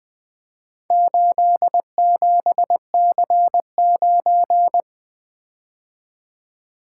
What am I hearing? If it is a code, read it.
Morse code, heard as a keyed tone, about 20 wpm, 700 Hz. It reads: 87C9
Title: Four Letter-Number Combinations - 20wpm